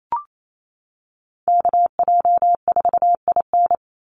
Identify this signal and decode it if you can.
Morse code, heard as a keyed tone, about 28 wpm, 700 Hz. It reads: KJ4IN
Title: USA Call Signs 28wpm